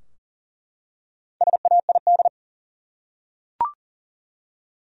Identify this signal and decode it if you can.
Morse code, heard as a keyed tone, about 39 wpm, 700 Hz. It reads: SAID
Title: Top 100 Words 40wpm